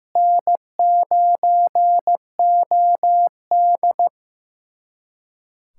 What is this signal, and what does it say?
Morse code, heard as a keyed tone, about 15 wpm, 700 Hz. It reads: N9OD
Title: USA Callsign DE Callsign - 4 Characters - 15wpm